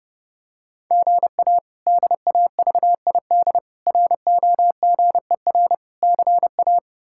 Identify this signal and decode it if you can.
Morse code, heard as a keyed tone, about 30 wpm, 700 Hz. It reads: GA DAVID ROGER CA
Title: K1USN SST QSO Practice - 30wpm